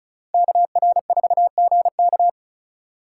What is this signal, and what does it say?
Morse code, heard as a keyed tone, about 34 wpm, 700 Hz. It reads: KR4GK
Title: USA Call Signs - 5 Characters - 2x Character Spacing - 35wpm